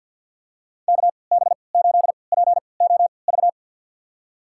Morse code, heard as a keyed tone, about 52 wpm, spaced as for 29 wpm, 700 Hz. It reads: KB8POV
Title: USA Call Signs - 3x Character Spacing - 50wpm